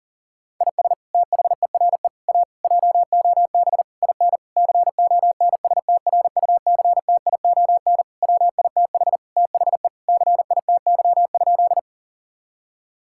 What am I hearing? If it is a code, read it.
Morse code, heard as a keyed tone, about 40 wpm, 700 Hz. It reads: IS THERE A JOB IN CONSTRUCTION WITH THE CITY?
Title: Sentences from Top 1600 Words 40wpm